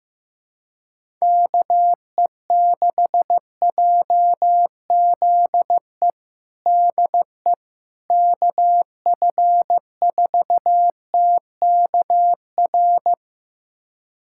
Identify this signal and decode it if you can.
Morse code, heard as a keyed tone, about 15 wpm, 700 Hz. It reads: KE6JZE DE KF4TKR
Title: USA Callsign DE Callsign - 6 Characters - 15wpm